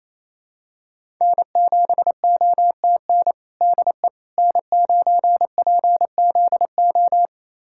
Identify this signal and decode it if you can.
Morse code, heard as a keyed tone, about 28 wpm, 700 Hz. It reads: N7OTN DE N9PZO
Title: USA Callsign DE Callsign - 5 Characters - 28wpm